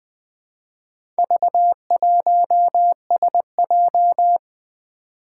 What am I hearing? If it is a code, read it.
Morse code, heard as a keyed tone, about 20 wpm, 700 Hz. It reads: V1SJ
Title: Four Challenging Letter-Number Combinations 20wpm